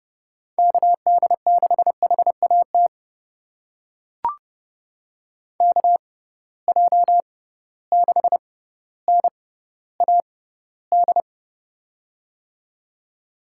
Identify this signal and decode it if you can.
Morse code, heard as a keyed tone, about 30 wpm, spaced as for 15 wpm, 700 Hz. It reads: KD6HAT KJ6NAD
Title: USA Call Signs - 6 Characters - 6x Character Spacing - 30wpm